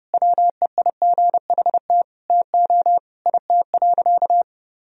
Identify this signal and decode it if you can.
Morse code, heard as a keyed tone, about 30 wpm, 700 Hz. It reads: WEIGHT TO IT.
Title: Sentences from Top 1000 Words - Review Part 3 - 30wpm